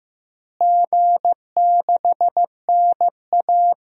Morse code, heard as a keyed tone, about 15 wpm, 700 Hz. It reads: G6NA
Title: DXCC-100 Call Signs - 4 Characters - Repeated 2x - 15wpm